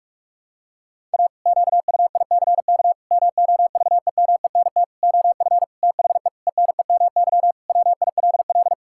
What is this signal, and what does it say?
Morse code, heard as a keyed tone, about 46 wpm, 700 Hz. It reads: A QUICK MOVEMENT OF THE ENEMY WILL
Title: Pangram Challenge v2 - 45wpm